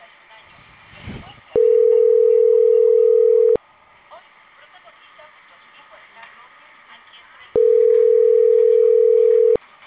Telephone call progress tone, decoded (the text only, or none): ringback tone